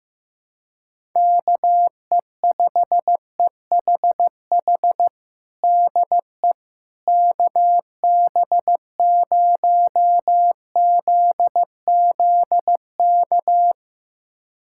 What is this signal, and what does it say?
Morse code, heard as a keyed tone, about 15 wpm, 700 Hz. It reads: KE5EHH DE KB0ZZK
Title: USA Callsign DE Callsign - 6 Characters - 15wpm